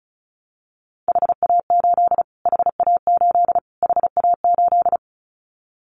Morse code, heard as a keyed tone, about 34 wpm, 700 Hz. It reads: HA8 HA8 HA8
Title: Callsign Prefixes for Top 25 DXCC - Repeated 3x - 35wpm